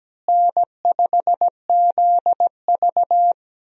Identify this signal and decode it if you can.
Morse code, heard as a keyed tone, about 17 wpm, 700 Hz. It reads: N5ZV
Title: USA Call Signs - 4 Characters - Repeated 3x - 17wpm